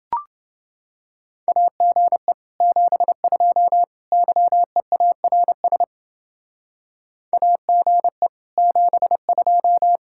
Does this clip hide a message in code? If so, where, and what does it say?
Yes: Morse code from 1.5 to 10.0 s, heard as a keyed tone, about 29 wpm, 700 Hz. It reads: AGE 72 YEARS AGE 72